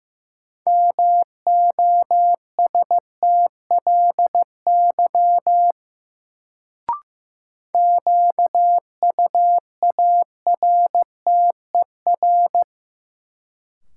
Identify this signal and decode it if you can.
Morse code, heard as a keyed tone, about 15 wpm, 700 Hz. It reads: MOSTLY QUARTER
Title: Sentences from Top 1300 Words - New Word Drill - 15wpm